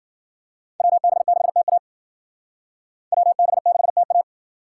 Morse code, heard as a keyed tone, about 52 wpm, 700 Hz. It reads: WB6TA WB6TA